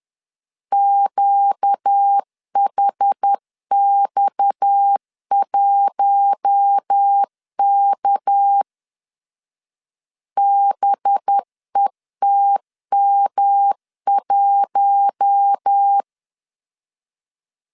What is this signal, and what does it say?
Morse code, heard as a keyed tone, about 10 wpm, 800 Hz. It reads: QHX1K BETM1
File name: LEZIONI / 6 PKX0 / 6 PKX 10wpm 2426